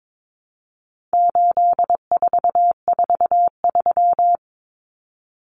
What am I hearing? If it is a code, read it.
Morse code, heard as a keyed tone, about 22 wpm, 700 Hz. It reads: 8443
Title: Common Cut Numbers - 4-Digit 22wpm